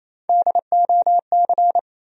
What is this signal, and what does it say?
Morse code, heard as a keyed tone, about 28 wpm, 700 Hz. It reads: DOC
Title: USA Call Sign Postfixes - 3 Letters 28wpm